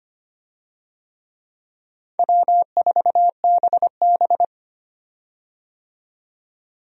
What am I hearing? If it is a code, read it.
Morse code, heard as a keyed tone, about 25 wpm, 700 Hz. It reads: W4BB
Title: USA Call Signs - 4 Characters - 25wpm